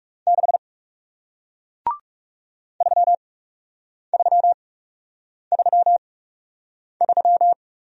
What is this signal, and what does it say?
Morse code, heard as a keyed tone, about 40 wpm, spaced as for 8 wpm, 700 Hz. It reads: 6 3333